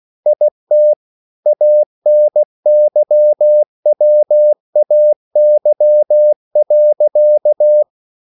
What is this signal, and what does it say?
Morse code, heard as a keyed tone, about 16 wpm, 600 Hz. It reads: IT ANYWAY.